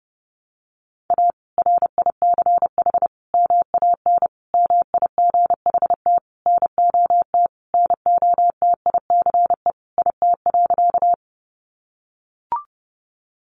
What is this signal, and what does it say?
Morse code, heard as a keyed tone, about 30 wpm, 700 Hz. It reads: A RICH MAN MIGHT NOT NOTICE IT.
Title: Sentences from Top 1100 Words 30wpm